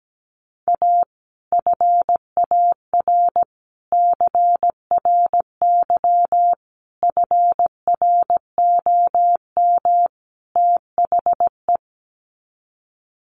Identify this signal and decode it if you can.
Morse code, heard as a keyed tone, about 17 wpm, 700 Hz. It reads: A FAR CRY FROM THE
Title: Sets of 5 Words Encore 17wpm